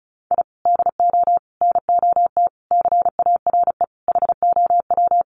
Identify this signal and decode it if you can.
Morse code, heard as a keyed tone, about 34 wpm, 700 Hz. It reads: I DO NOT CARE HOW
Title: Sets of 5 Words Encore2 35wpm